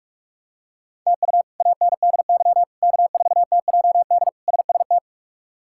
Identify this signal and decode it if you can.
Morse code, heard as a keyed tone, about 43 wpm, 700 Hz. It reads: TU ANDY K4TJD SST
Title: K1USN SST QSO Practice - 45wpm